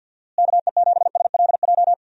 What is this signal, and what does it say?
Morse code, heard as a keyed tone, about 46 wpm, 700 Hz. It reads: KE7SLJ
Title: USA Call Signs - 6 Characters - Repeated 3x - 50wpm